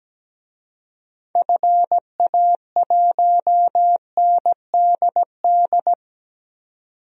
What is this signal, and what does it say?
Morse code, heard as a keyed tone, about 17 wpm, 700 Hz. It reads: FA1NDD
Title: DXCC-100 Call Signs - 6 Characters - 17wpm